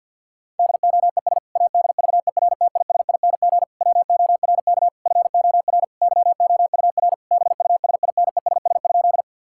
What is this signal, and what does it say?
Morse code, heard as a keyed tone, about 50 wpm, 700 Hz. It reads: DOES ADVERTISING WORK FOR YOUR BUSINESS?
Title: Sentences from Top 1000 Words - Review Part 6 - 50wpm